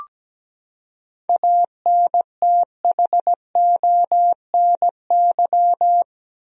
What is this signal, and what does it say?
Morse code, heard as a keyed tone, about 17 wpm, 700 Hz. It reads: ANTHONY
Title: USA Common Names - 17wpm